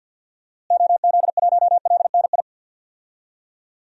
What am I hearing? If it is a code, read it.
Morse code, heard as a keyed tone, about 50 wpm, 700 Hz. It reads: OZ1LNI
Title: DXCC-100 Call Signs - 6 Characters - Repeated 3x - 50wpm